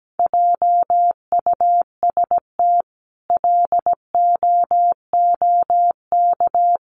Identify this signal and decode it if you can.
Morse code, heard as a keyed tone, about 17 wpm, 700 Hz. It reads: JUST LOOK
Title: Sentences from Top 200 Words 17wpm